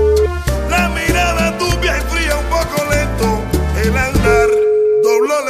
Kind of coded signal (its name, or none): telephone call progress tone